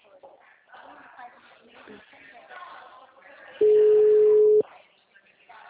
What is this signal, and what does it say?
Ringback tone (Europe, 425 Hz), with speech.
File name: out- -2015- - - .14587